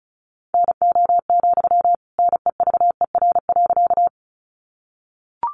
Morse code, heard as a keyed tone, about 34 wpm, 700 Hz. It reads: NO, NEVER.
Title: Sentences from Top 300 Words 35wpm